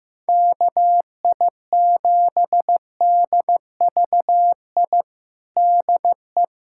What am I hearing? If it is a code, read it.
Morse code, heard as a keyed tone, about 16 wpm, 700 Hz. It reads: KI7DVI DE
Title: USA Callsign DE Callsign - 6 Characters - 15wpm